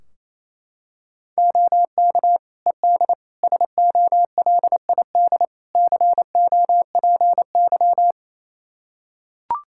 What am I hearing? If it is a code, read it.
Morse code, heard as a keyed tone, about 28 wpm, 700 Hz. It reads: OK ED SOLID COPY
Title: Practice QSO - Exchange 3 - More Info - 28wpm